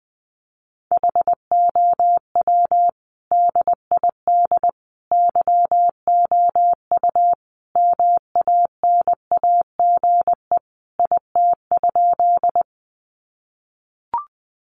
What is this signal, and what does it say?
Morse code, heard as a keyed tone, about 20 wpm, 700 Hz. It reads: HOW DID YOU MANAGE IT?
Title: Sentences from Top 1400 Words 20wpm